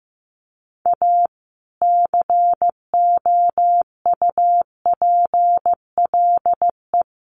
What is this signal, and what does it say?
Morse code, heard as a keyed tone, about 15 wpm, 700 Hz. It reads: A COUPLE
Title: Sets of 5 Words 15wpm